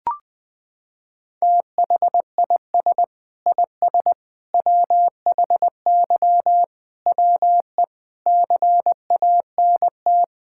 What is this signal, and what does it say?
Morse code, heard as a keyed tone, about 21 wpm, 700 Hz. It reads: THIS IS WHY WE CANT
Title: Sets of 8 Words - 20wpm